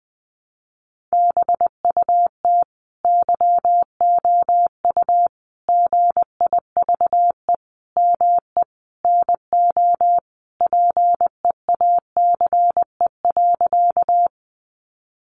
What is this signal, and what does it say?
Morse code, heard as a keyed tone, about 20 wpm, 700 Hz. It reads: BUT YOU GIVE ME NO PEACE.
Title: Sentences from Top 1500 Words 20wpm